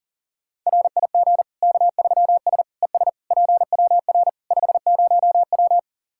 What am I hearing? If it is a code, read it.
Morse code, heard as a keyed tone, about 40 wpm, 700 Hz. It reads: RIG K3S ES PWR 50W
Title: K1USN SST QSO Practice 40wpm